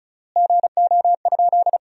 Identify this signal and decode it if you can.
Morse code, heard as a keyed tone, about 34 wpm, 700 Hz. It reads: GO?
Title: 3x Word-Spacing - Sentences from Top 300 Words - 35wpm